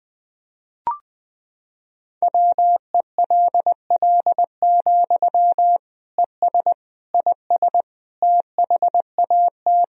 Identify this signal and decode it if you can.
Morse code, heard as a keyed tone, about 20 wpm, 700 Hz. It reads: WELL, ES IS THAT
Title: Sentences from Top 200 Words and Abbreviations 20wpm